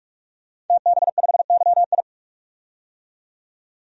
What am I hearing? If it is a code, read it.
Morse code, heard as a keyed tone, about 42 wpm, 700 Hz. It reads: TB5YI
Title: DXCC-100 Call Signs - 5 Characters - 45wpm